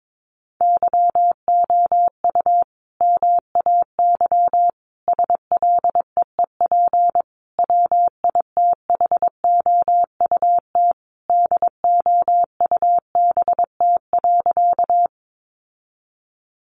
Morse code, heard as a keyed tone, about 22 wpm, 700 Hz. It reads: YOU MAY SLEEP WITHOUT DOUBT.
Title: Sentences from Top 1500 Words 22wpm